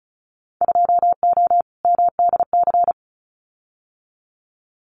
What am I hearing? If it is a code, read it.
Morse code, heard as a keyed tone, about 34 wpm, 700 Hz. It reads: 2O MDC
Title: Winter Field Day Exchanges 35wpm